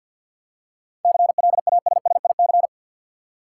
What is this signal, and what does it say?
Morse code, heard as a keyed tone, about 46 wpm, 700 Hz. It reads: CLASSIC
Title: Seven Letter Words - 50wpm